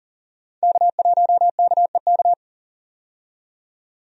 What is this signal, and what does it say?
Morse code, heard as a keyed tone, about 40 wpm, 700 Hz. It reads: K1KEK
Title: USA Call Signs - 5 Characters - 40wpm